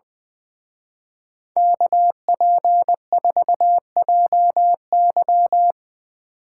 Morse code, heard as a keyed tone, about 22 wpm, 700 Hz. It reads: KP4JY
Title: Speed Racing - USA Call Signs 20wpm